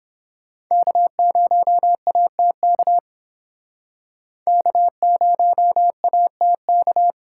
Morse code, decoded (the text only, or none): K0ATK K0ATK